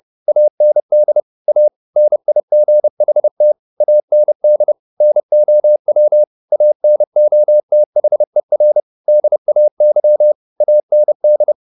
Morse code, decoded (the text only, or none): AND A NIGHT AND NOW ANOTHER DAY AND